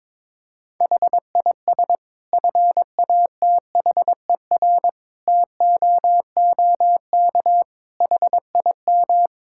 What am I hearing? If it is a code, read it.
Morse code, heard as a keyed tone, about 22 wpm, 700 Hz. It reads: HIS FATHER TOOK HIM
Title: Sentences from Top 1000 Words - Review Part 5 - 22wpm